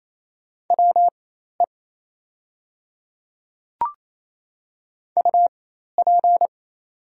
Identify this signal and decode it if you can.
Morse code, heard as a keyed tone, about 27 wpm, spaced as for 13 wpm, 690 Hz. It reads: WE UP
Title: 4x Character Spacing - Top 300 Words - 28wpm